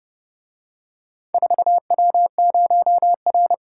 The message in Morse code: 4W0R

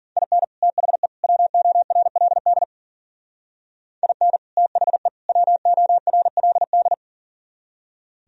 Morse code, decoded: IN THE WORLD IN THE WORLD

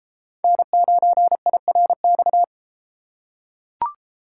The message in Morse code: N9IRX

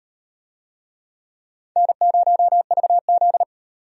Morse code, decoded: N0VZ

